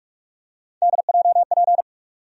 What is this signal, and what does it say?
Morse code, heard as a keyed tone, about 46 wpm, 700 Hz. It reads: DJP
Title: USA Call Sign Postfixes - 3 Letters 45wpm